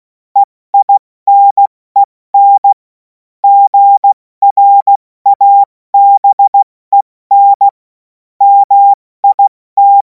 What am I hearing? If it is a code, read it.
Morse code, heard as a keyed tone, about 16 wpm, 800 Hz. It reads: EINEN GRABEN MIT
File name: Morsekurs / Witze / Witz 17. Februar 2026 70 F80